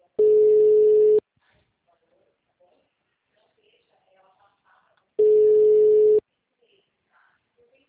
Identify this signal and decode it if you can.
Ringback tone (Europe, 425 Hz), with speech.
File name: out- -2013- - - .194